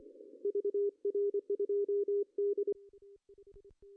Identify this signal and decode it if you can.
Morse code, heard as a keyed tone, about 26 wpm, 410 Hz. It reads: VR2D
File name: CW / cqww-cw-k5tr